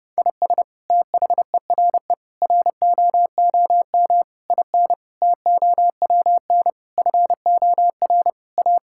Morse code, heard as a keyed tone, about 30 wpm, 700 Hz. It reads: IS THERE ROOM IN TOWN FOR A
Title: Sentences from Top 1000 Words - Review Part 5 - 30wpm